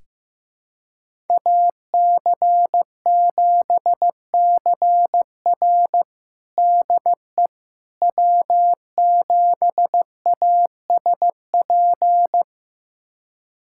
Morse code, heard as a keyed tone, about 15 wpm, 700 Hz. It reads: AC7CR DE W7ASP
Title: USA Callsign DE Callsign - 5 Characters - 15wpm